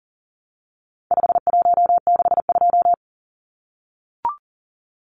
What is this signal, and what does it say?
Morse code, heard as a keyed tone, about 40 wpm, 700 Hz. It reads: 5162
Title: Numbers - 4-Digit 40wpm